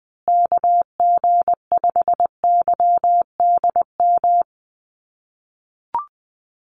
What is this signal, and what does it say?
Morse code, heard as a keyed tone, about 20 wpm, 700 Hz. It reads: KG5YDM
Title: USA Call Signs Repeated 3x 20wpm